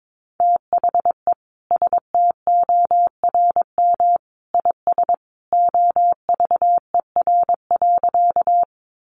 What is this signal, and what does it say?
Morse code, heard as a keyed tone, about 22 wpm, 700 Hz. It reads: THE STORM IS OVER.
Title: Sentences from Top 1500 Words 22wpm